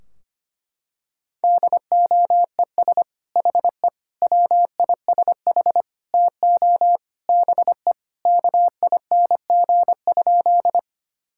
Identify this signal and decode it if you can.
Morse code, heard as a keyed tone, about 25 wpm, 700 Hz. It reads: DOES HE WISH TO BE KING?
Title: Sentences from Top 1600 Words 25wpm